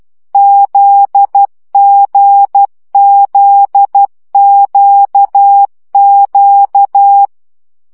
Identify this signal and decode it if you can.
Morse code, heard as a keyed tone, about 12 wpm, 800 Hz. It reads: ZGZQQ
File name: CW gyakorlas g z q 10 baud